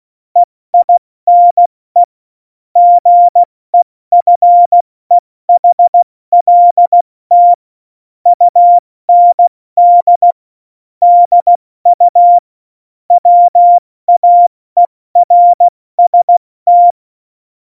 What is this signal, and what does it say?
Morse code, heard as a keyed tone, about 16 wpm, 700 Hz. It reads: EINE GEFEHLT UND DU WAERST